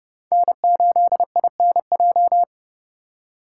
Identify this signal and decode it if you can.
Morse code, heard as a keyed tone, about 28 wpm, 700 Hz. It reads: N8INJ